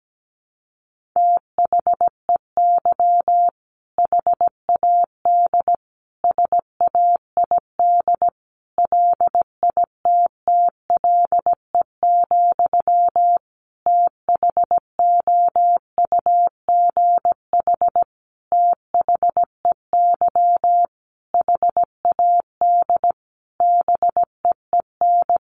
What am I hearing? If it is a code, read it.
Morse code, heard as a keyed tone, about 17 wpm, 700 Hz. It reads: THEY HAD SAID LITTLE, THOUGH THEY HAD BEEN
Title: Sentences from Top 400 Words 17wpm